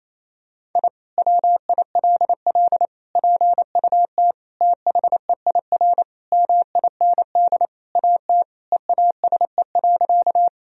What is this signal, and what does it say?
Morse code, heard as a keyed tone, about 28 wpm, 700 Hz. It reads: I WILL PUT THEIR MIND AT EASE.
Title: Sentences from Top 1000 Words - Review Part 1 - 28wpm